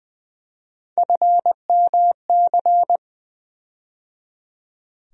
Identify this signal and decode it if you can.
Morse code, heard as a keyed tone, about 20 wpm, 700 Hz. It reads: FMC
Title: USA Call Sign Postfixes - 3 Letters 20wpm